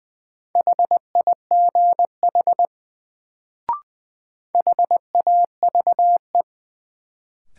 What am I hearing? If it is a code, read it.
Morse code, heard as a keyed tone, about 20 wpm, 700 Hz. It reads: HIGH HAVE
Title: Sets of 2 Words - New Word Drill - 20wpm